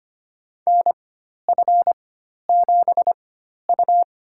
Morse code, heard as a keyed tone, about 25 wpm, spaced as for 12 wpm, 700 Hz. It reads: NF7U
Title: USA Call Signs - 4x Character Spacing - 25wpm